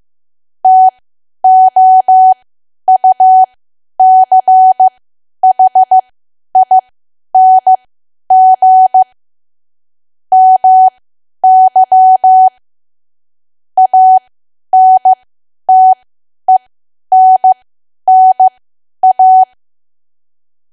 Morse code, heard as a keyed tone, about 15 wpm, spaced as for 10 wpm, 750 Hz. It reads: TOUCHING MY ANTENNA
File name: Archive / 10 WPM / 10WPM